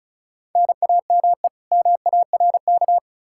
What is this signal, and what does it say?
Morse code, heard as a keyed tone, about 36 wpm, 700 Hz. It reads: NAME MARK
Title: Practice QSO - Exchange 1 - Beginning - 35wpm